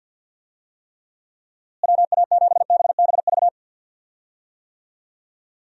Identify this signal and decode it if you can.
Morse code, heard as a keyed tone, about 51 wpm, 700 Hz. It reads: WA7BBV
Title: USA Call Signs - 6 Characters - Repeated 3x - 50wpm